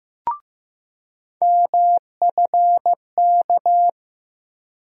Morse code, heard as a keyed tone, about 15 wpm, 700 Hz. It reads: MFK
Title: USA Call Sign Postfixes - 3 Letters 15wpm